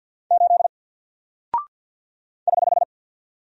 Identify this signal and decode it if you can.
Morse code, heard as a keyed tone, about 49 wpm, 700 Hz. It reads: 8 <HH>
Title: Single Character (Letter:Number:Punctuation:Prosign) Alt 50wpm